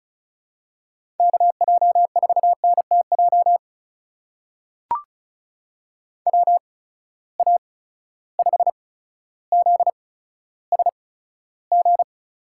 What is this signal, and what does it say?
Morse code, heard as a keyed tone, about 36 wpm, spaced as for 15 wpm, 700 Hz. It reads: KJ4NTJ WA5ZSG
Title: USA Call Signs - 6 Characters - 8x Character Spacing - 35wpm